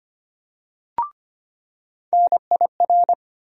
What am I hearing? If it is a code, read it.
Morse code, heard as a keyed tone, about 25 wpm, 700 Hz. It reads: NIR
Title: USA Call Sign Postfixes - 3 Letters 25wpm